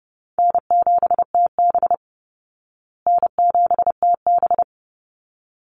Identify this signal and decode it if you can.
Morse code, heard as a keyed tone, about 30 wpm, 700 Hz. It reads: N7TB N7TB
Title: Call Signs Repeated 3x - 30wpm